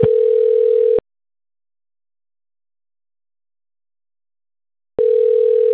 Ringback tone (North America), 440 + 480 Hz.
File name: rg-1600-unknown- - - .49020